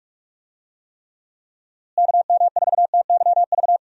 Morse code, heard as a keyed tone, about 46 wpm, 700 Hz. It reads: KM4TYV
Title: USA Call Signs - 6 Characters - Repeated 3x - 45wpm